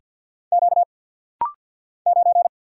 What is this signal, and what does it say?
Morse code, heard as a keyed tone, about 50 wpm, 700 Hz. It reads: Q 9
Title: Single Letter-Number Alt - Rapid-Fire 50wpm